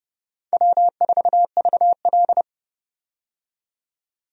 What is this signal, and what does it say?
Morse code, heard as a keyed tone, about 30 wpm, 700 Hz. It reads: W4VL
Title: USA Call Signs - 4 Characters - 30wpm